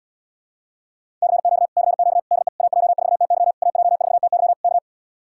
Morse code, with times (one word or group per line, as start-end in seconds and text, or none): 1.2-1.7 CQ
1.8-2.2 CQ
2.3-2.5 DE
2.6-3.5 AJ4TY
3.6-4.5 AJ4TY
4.6-4.8 K